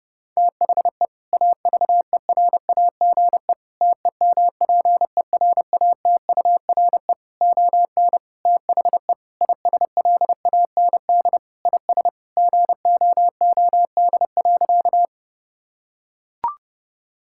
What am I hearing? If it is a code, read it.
Morse code, heard as a keyed tone, about 28 wpm, 700 Hz. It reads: THE AVERAGE TEMPERATURE ON THE ISLAND IS GOOD.
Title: Sentences from Top 1000 Words - Review Part 6 - 30wpm